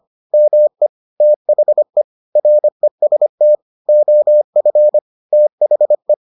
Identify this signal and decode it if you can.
Morse code, heard as a keyed tone, about 25 wpm, 600 Hz. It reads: ME THE REST OF THE